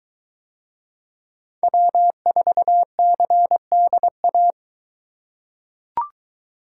Morse code, heard as a keyed tone, about 22 wpm, 700 Hz. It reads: W4CDA